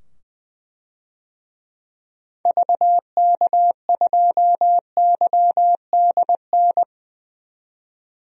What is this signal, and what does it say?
Morse code, heard as a keyed tone, about 20 wpm, 700 Hz. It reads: VK2YDN
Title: DXCC-100 Call Signs - 6 Characters - 20wpm